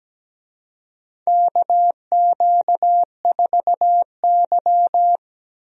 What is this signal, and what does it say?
Morse code, heard as a keyed tone, about 17 wpm, 700 Hz. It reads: KQ4Y
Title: USA Call Signs - 4 Characters - Repeated 2x - 17wpm